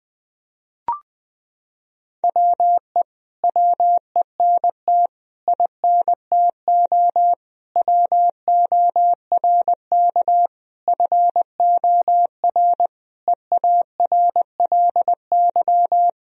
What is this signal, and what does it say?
Morse code, heard as a keyed tone, about 20 wpm, 700 Hz. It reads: WE WENT INTO WORK FOR EARLY
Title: Sentences from Top 1000 Words - Review Part 4 - 20wpm